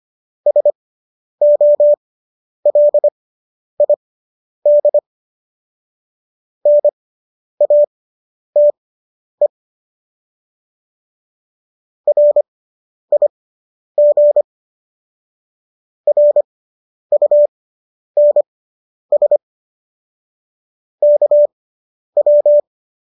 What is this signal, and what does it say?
Morse code, heard as a keyed tone, about 25 wpm, spaced as for 10 wpm, 590 Hz. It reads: SOLID NATE RIG RUNS KW